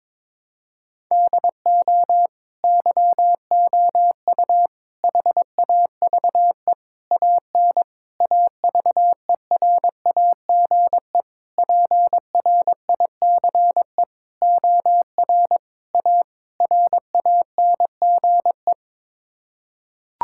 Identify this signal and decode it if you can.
Morse code, heard as a keyed tone, about 22 wpm, 700 Hz. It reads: DO YOU HAVE AN AVERAGE PRICE OR A RANGE
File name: Sets of 9 Words - 22wpm